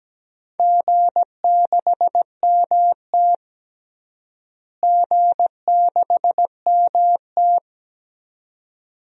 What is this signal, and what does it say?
Morse code, heard as a keyed tone, about 17 wpm, 700 Hz. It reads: G6MT G6MT
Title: DXCC-100 Call Signs - 4 Characters - Repeated 2x - 17wpm